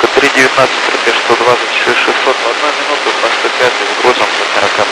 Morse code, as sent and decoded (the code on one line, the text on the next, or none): none